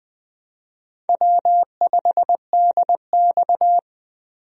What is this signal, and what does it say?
Morse code, heard as a keyed tone, about 20 wpm, 700 Hz. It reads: W5DX
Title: USA Call Signs - 4 Characters - Repeated 2x - 20wpm